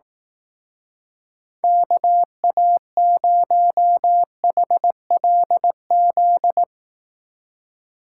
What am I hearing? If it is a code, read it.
Morse code, heard as a keyed tone, about 18 wpm, 700 Hz. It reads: KA0HLZ
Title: Speed Racing - USA 2x3 Call Signs 15wpm